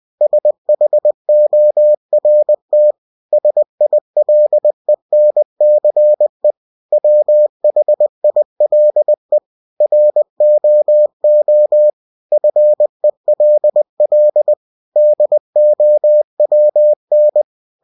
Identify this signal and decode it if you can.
Morse code, heard as a keyed tone, about 20 wpm, 600 Hz. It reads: SHORT SILENCE WHILE ROO FELL DOWN